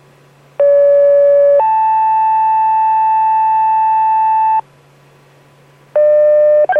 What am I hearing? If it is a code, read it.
Two-tone page, heard as a one-second tone, then a three-second tone, about 570 then 880 Hz. It is heard from 0.6 to 4.6 s.